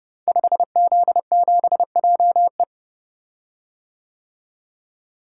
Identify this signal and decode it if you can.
Morse code, heard as a keyed tone, about 30 wpm, 700 Hz. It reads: 5Z7JE
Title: Code Groups - 5 Letter:Numbers - 30wpm